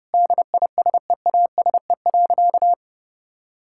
Morse code, heard as a keyed tone, about 30 wpm, 700 Hz. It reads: DISEASE.
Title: Sentences from Top 1000 Words - Review Part 6 - 30wpm